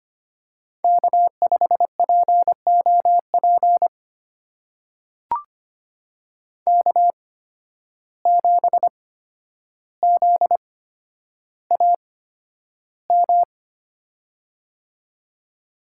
Morse code, heard as a keyed tone, about 25 wpm, spaced as for 11 wpm, 700 Hz. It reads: K5POP K7ZAM
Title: USA Call Signs - 5 Characters - 8x Character Spacing - 25wpm